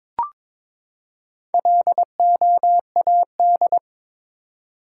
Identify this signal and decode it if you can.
Morse code, heard as a keyed tone, about 22 wpm, 700 Hz. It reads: LOAD
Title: Four Letter Words - 22wpm